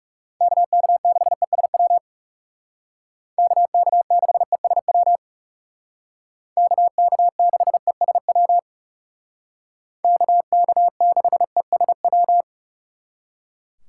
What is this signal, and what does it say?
Morse code, heard as a keyed tone, about 35 wpm, 700 Hz. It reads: KK6ESW KK6ESW KK6ESW KK6ESW
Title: Speed Racing - USA 2x3 Call Signs 30wpm